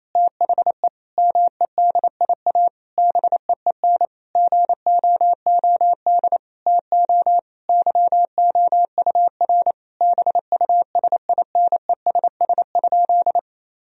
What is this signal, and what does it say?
Morse code, heard as a keyed tone, about 28 wpm, 700 Hz. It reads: THE MEDIA BEEN GOOD TO YOUR BUSINESS?
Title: Sentences from Top 1000 Words - Review Part 4 - 28wpm